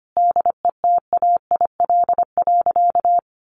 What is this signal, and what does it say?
Morse code, heard as a keyed tone, about 25 wpm, 700 Hz. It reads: DETAIL.
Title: Sentences from Top 1500 Words 25wpm